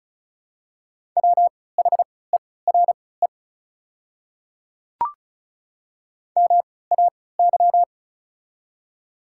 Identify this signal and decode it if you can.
Morse code, heard as a keyed tone, about 34 wpm, spaced as for 20 wpm, 700 Hz. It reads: WHERE MAY
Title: 3x Character Spacing - Top 200 Words - 35wpm